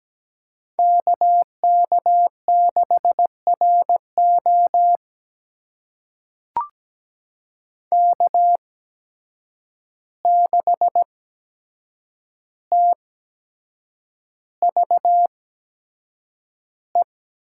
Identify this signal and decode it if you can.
Morse code, heard as a keyed tone, about 17 wpm, spaced as for 7 wpm, 700 Hz. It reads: KK6RO K6TVE
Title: USA Call Signs - 5 Characters - 8x Character Spacing - 17wpm